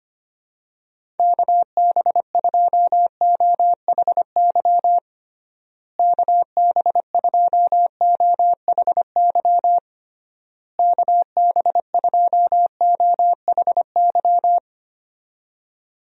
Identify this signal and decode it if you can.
Morse code, heard as a keyed tone, about 25 wpm, 700 Hz. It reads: KB2OHY KB2OHY KB2OHY
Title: USA Call Signs - 6 Characters - Repeated 3x - 25wpm